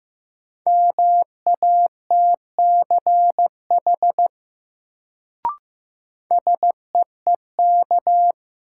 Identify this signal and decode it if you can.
Morse code, heard as a keyed tone, about 15 wpm, 700 Hz. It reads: MATCH SEEK
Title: Sentences from Top 1600 Words - New Word Drill - 15wpm